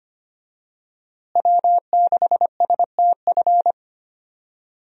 Morse code, heard as a keyed tone, about 25 wpm, 700 Hz. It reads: W6STF
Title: USA Call Signs - 5 Characters - Repeated 2x - 25wpm